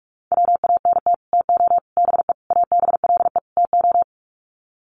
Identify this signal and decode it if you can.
Morse code, heard as a keyed tone, about 44 wpm, 700 Hz. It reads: WANT TO BE ABLE TO
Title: Sets of 5 Words 45wpm